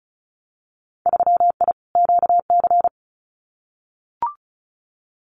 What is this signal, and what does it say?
Morse code, heard as a keyed tone, about 36 wpm, 700 Hz. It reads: 3I QC
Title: Winter Field Day Exchanges Encore 35wpm